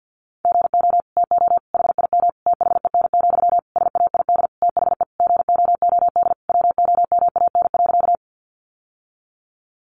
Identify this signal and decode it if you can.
Morse code, heard as a keyed tone, about 50 wpm, 700 Hz. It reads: GO TO HIM THEN, SAID THE GOOD WOMAN.
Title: Sentences from Top 1100 Words 50wpm